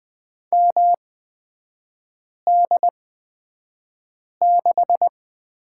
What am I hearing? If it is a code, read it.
Morse code, heard as a keyed tone, about 20 wpm, spaced as for 5 wpm, 700 Hz. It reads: MD6
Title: 243 Lesson 39e - Letter z - Three Characters 20wpm